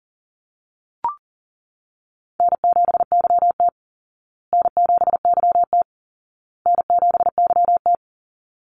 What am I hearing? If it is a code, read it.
Morse code, heard as a keyed tone, about 39 wpm, 700 Hz. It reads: N7YT N7YT N7YT
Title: Call Signs Repeated 2x - 40wpm